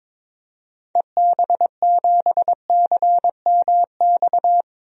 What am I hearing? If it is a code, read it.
Morse code, heard as a keyed tone, about 21 wpm, 700 Hz. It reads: EB7CMX
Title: DXCC-100 Call Signs - 6 Characters - 22wpm